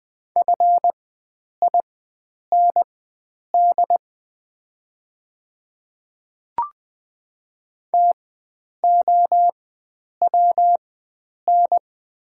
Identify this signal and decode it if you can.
Morse code, heard as a keyed tone, about 20 wpm, spaced as for 9 wpm, 700 Hz. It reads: FIND TOWN
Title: 4x Character Spacing - Top 500 Words - 20wpm